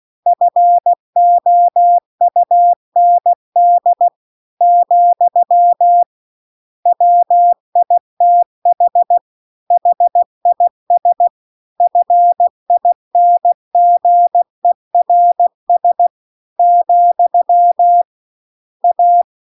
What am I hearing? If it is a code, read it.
Morse code, heard as a keyed tone, about 16 wpm, 700 Hz. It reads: FOUND , WITH HIS FINGERS , A